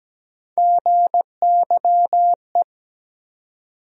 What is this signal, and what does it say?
Morse code, heard as a keyed tone, about 17 wpm, 700 Hz. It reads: GYE
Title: USA Call Sign Postfixes - 3 Letters 17wpm